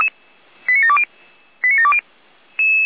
Five-tone selective call (ZVEI), heard as quick runs of five tones, tones of about 70 ms. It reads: 89720 89720